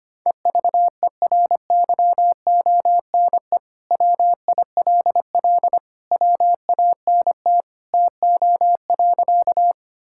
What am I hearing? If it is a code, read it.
Morse code, heard as a keyed tone, about 25 wpm, 700 Hz. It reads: EVERYONE WILL WANT TO.
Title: Sentences from Top 1000 Words - Review Part 5 - 25wpm